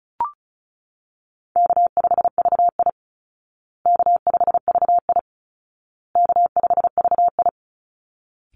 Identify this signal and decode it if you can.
Morse code, heard as a keyed tone, about 35 wpm, 700 Hz. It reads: K5VI K5VI K5VI
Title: Easy Call Signs Repeated 3x - 35wpm